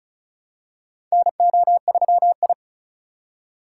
Morse code, heard as a keyed tone, about 36 wpm, 700 Hz. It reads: NO3I
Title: USA Call Signs - 4 Characters - 4x Character Spacing - 35wpm